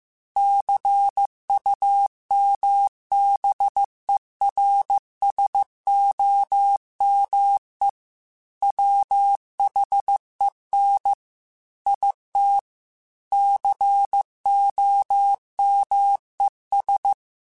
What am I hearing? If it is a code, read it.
Morse code, heard as a keyed tone, about 15 wpm, 800 Hz. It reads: CUMBERSOME WHEN IT COMES
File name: Technical Subjects / Technical Subjects-mp3s / AND Gate @13